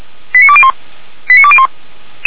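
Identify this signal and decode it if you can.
Five-tone selective call (ZVEI), heard as quick runs of five tones, tones of about 70 ms. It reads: 89201 89201
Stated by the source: ZVEI five-tone selective call received by SDR.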